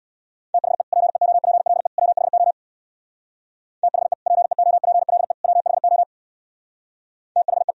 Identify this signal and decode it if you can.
Morse code, heard as a keyed tone, about 63 wpm, 700 Hz. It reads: THE PEOPLE WHO THE PEOPLE WHO THE